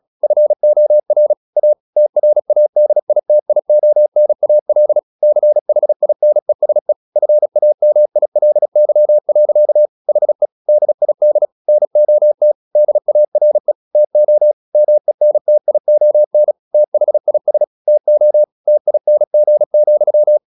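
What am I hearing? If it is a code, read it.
Morse code, heard as a keyed tone, about 36 wpm, 600 Hz. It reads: FOR A TRADITIONAL CHINESE FAMILY. HE DID NOT DARE TO MENTION THIS TO TING,
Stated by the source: Morse practice at 36 wpm, text of Ken Follett's novel Never, from CW med Gnister.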